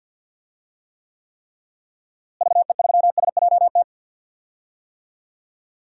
Morse code, heard as a keyed tone, about 51 wpm, 700 Hz. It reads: VE3SJT
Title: DXCC-100 Call Signs - 6 Characters - Repeated 3x - 50wpm